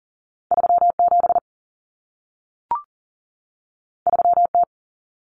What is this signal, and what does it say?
Morse code, heard as a keyed tone, about 40 wpm, 700 Hz. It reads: 37 3T
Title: Common Cut Numbers - 2-Digit 40wpm